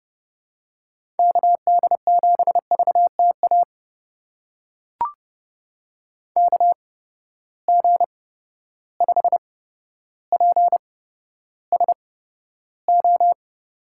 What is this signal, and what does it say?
Morse code, heard as a keyed tone, about 30 wpm, spaced as for 13 wpm, 700 Hz. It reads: KD7VTA KG5PSO
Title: USA Call Signs - 6 Characters - 8x Character Spacing - 30wpm